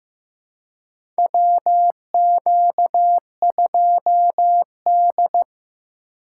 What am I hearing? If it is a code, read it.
Morse code, heard as a keyed tone, about 15 wpm, 700 Hz. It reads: WQ2D
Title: USA Call Signs - 4 Characters - 3x Character Spacing - 15wpm